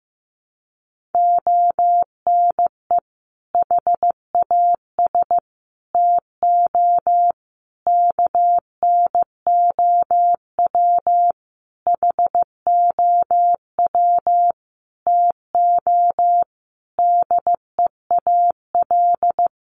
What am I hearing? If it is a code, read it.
Morse code, heard as a keyed tone, about 15 wpm, 700 Hz. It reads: ONE HAS TO KNOW HOW TO DEAL
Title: Sentences from Top 800 Words 15wpm